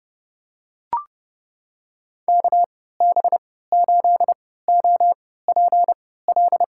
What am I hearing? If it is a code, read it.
Morse code, heard as a keyed tone, about 32 wpm, spaced as for 17 wpm, 700 Hz. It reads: KB8OPL
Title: USA Call Signs - 6 Characters - 3x Character Spacing - 30wpm